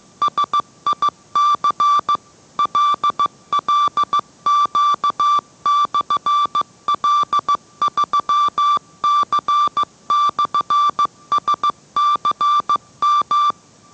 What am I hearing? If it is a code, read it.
Morse code, heard as a keyed tone, about 15 wpm, 1170 Hz. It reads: SIC LLQ/L3C/SCM